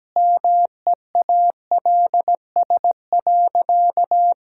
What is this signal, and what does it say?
Morse code, heard as a keyed tone, about 17 wpm, 700 Hz. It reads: MEALS.
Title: Pangram Challenge v2 - 17wpm